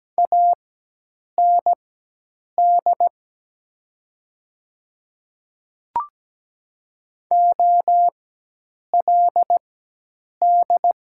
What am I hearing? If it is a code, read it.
Morse code, heard as a keyed tone, about 17 wpm, spaced as for 8 wpm, 700 Hz. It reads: AND OLD
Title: 4x Character Spacing - Top 200 Words - 17wpm